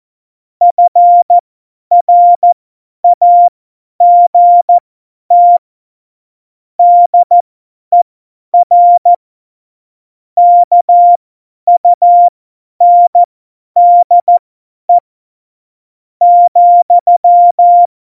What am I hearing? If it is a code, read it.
Morse code, heard as a keyed tone, about 14 wpm, spaced as for 9 wpm, 700 Hz. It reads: FRAGT DER KUNDE,
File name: Morsekurs / Witze / Witz 15. Januar 2019, 50 F70